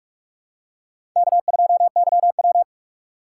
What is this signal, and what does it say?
Morse code, heard as a keyed tone, about 43 wpm, 700 Hz. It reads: K2YW
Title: USA Call Signs - 4 Characters - 4x Character Spacing - 45wpm